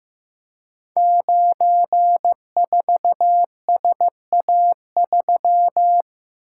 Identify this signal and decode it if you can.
Morse code, heard as a keyed tone, about 15 wpm, 700 Hz. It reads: 94SA3
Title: Code Groups - 5 Letter:Numbers - 15wpm